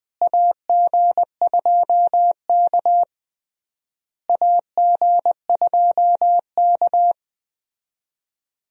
Morse code, heard as a keyed tone, about 20 wpm, 700 Hz. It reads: AG2K AG2K
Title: USA Call Signs - 4 Characters - Repeated 3x - 20wpm